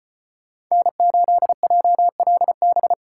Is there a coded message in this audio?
Yes: Morse code from 0.7 to 2.9 s, heard as a keyed tone, about 33 wpm, 700 Hz. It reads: N8JLB